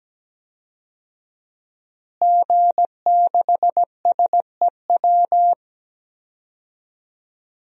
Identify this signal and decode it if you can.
Morse code, heard as a keyed tone, about 17 wpm, 700 Hz. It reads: G6SEW
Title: DXCC-100 Call Signs - 5 Characters - Repeated 2x - 17wpm